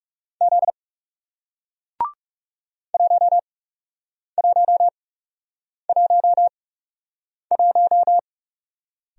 Morse code, heard as a keyed tone, about 40 wpm, spaced as for 8 wpm, 700 Hz. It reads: Z 1111